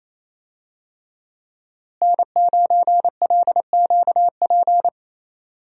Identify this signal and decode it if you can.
Morse code, heard as a keyed tone, about 27 wpm, 700 Hz. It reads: N9LQP
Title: USA Call Signs - 5 Characters - Repeated 3x - 28wpm